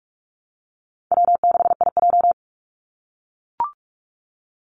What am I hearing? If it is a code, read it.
Morse code, heard as a keyed tone, about 44 wpm, 700 Hz. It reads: W6IJ
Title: Call Signs - 45wpm